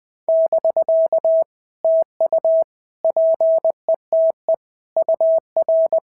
Morse code, heard as a keyed tone, about 19 wpm, 650 Hz. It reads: <BK> TU PETE UR
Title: POTA QSO Practice - 20wpm